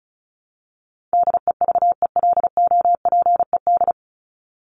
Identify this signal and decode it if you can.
Morse code, heard as a keyed tone, about 34 wpm, 700 Hz. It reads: DEVELOPED
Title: ED Suffix - Top 100 Words with Base Words 35wpm